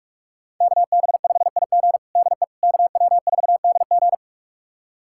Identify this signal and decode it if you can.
Morse code, heard as a keyed tone, about 46 wpm, 700 Hz. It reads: KB5IG DE KW4DG
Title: USA Callsign DE Callsign - 5 Characters - 45wpm